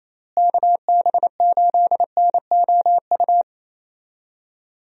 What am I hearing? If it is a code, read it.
Morse code, heard as a keyed tone, about 28 wpm, 700 Hz. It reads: KB8NOU
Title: USA Call Signs - 6 Characters - Repeated 2x - 28wpm